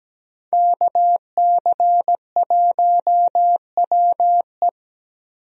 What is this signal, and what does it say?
Morse code, heard as a keyed tone, about 17 wpm, 700 Hz. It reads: KC1WE
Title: USA Call Signs - 5 Characters - Repeated 2x - 17wpm